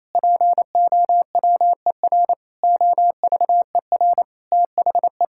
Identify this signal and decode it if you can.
Morse code, heard as a keyed tone, about 28 wpm, 700 Hz. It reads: POWER OVER THE
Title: Sentences from Top 1000 Words - Review Part 6 - 28wpm